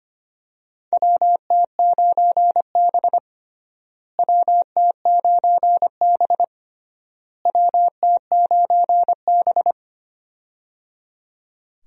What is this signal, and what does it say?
Morse code, heard as a keyed tone, about 25 wpm, 700 Hz. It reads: WT9B WT9B WT9B
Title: USA Call Signs - 4 Characters - Repeated 3x - 25wpm